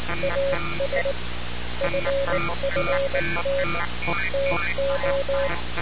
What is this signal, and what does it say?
Morse code, heard as a keyed tone, about 19 wpm, 550 Hz, with noise. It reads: AS FFT TO
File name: radiocap / REC z